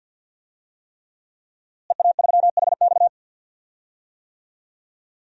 Morse code, heard as a keyed tone, about 51 wpm, 700 Hz. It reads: EA3HX
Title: DXCC-100 Call Signs - 5 Characters - Repeated 3x - 50wpm